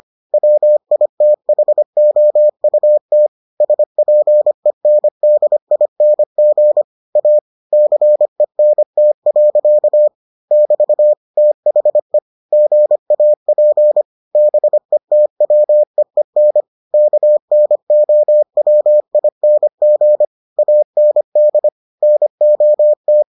Morse code, heard as a keyed tone, about 25 wpm, 600 Hz. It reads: WITHOUT SPENDING A CENT. <BT> THE GAP BETWEEN KNOWING AND NOT